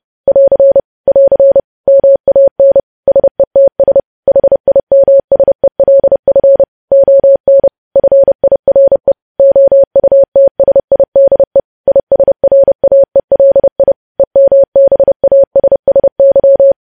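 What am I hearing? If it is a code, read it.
Morse code, heard as a keyed tone, about 30 wpm, 550 Hz. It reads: <AR> <AR> MAN SETS HIMSELF ON FIRE OUTSIDE ISRAELI EMBASSY
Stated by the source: News headlines sent in Morse at 30 words per minute.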